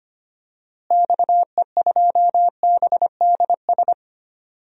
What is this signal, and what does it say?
Morse code, heard as a keyed tone, about 25 wpm, 700 Hz. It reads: XE2BDS
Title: DXCC-100 Call Signs - 6 Characters - Repeated 3x - 25wpm